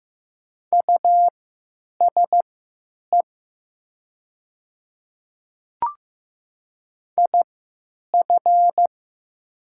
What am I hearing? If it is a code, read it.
Morse code, heard as a keyed tone, about 15 wpm, spaced as for 9 wpm, 700 Hz. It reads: USE IF
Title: 3x Character Spacing - Top 100 Words - 15wpm